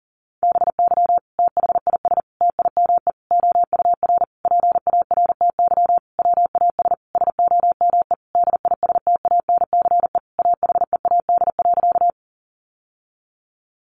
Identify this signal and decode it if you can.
Morse code, heard as a keyed tone, about 40 wpm, 700 Hz. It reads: BY THIS TIME OUR PARTY WAS SOME DISTANCE AHEAD.
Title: Sentences from Top 1400 Words 40wpm